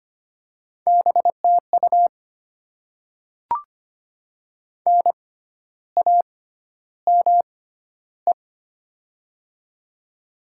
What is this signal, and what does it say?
Morse code, heard as a keyed tone, about 25 wpm, spaced as for 12 wpm, 700 Hz. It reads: BTU NAME
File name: Top 20 QSO Elements - 6x Character Spacing - 25wpm